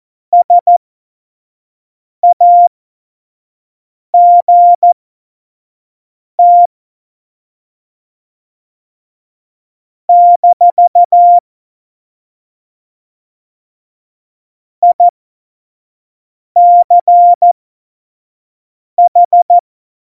Morse code, heard as a keyed tone, about 14 wpm, spaced as for 5 wpm, 700 Hz. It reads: SAGT - ICH
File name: Morsekurs / Witze / Witz 02. Oktober 2018, 25 F70